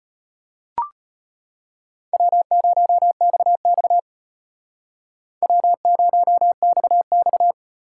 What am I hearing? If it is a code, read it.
Morse code, heard as a keyed tone, about 32 wpm, 700 Hz. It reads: W0XX W0XX